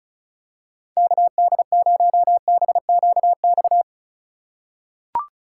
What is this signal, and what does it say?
Morse code, heard as a keyed tone, about 34 wpm, 700 Hz. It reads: KD0BQX
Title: USA Call Signs - 6 Characters - 4x Character Spacing - 35wpm